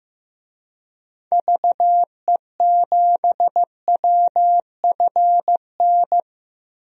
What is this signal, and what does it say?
Morse code, heard as a keyed tone, about 16 wpm, 700 Hz. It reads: VE7WFN
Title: DXCC-100 Call Signs - 6 Characters - Repeated 2x - 15wpm